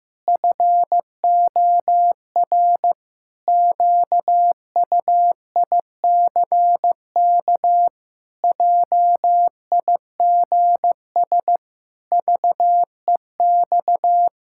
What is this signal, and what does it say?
Morse code, heard as a keyed tone, about 15 wpm, 700 Hz. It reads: FOR QUICK JIGS VEX
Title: Pangram Challenge v2 - 15wpm